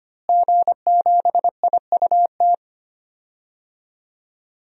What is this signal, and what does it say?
Morse code, heard as a keyed tone, about 25 wpm, 700 Hz. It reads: G7IUT
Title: DXCC-100 Call Signs - 5 Characters - Repeated 2x - 25wpm